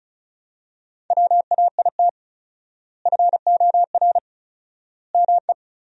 Morse code, heard as a keyed tone, about 36 wpm, 700 Hz. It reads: WAIT FOR ME
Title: 4x Word-Spacing - Sentences from Top 500 Words - 35wpm